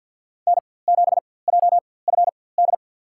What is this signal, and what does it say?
Morse code, heard as a keyed tone, about 48 wpm, spaced as for 23 wpm, 700 Hz. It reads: N7JFD
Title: USA Call Signs - 5 Characters - 4x Character Spacing - 50wpm